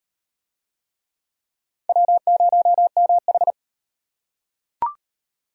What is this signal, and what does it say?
Morse code, heard as a keyed tone, about 37 wpm, 700 Hz. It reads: W0MH